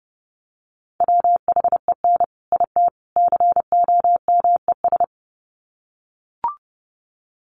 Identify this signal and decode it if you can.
Morse code, heard as a keyed tone, about 30 wpm, 700 Hz. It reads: WHEN IT COMES
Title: Sets of 3 Words 30wpm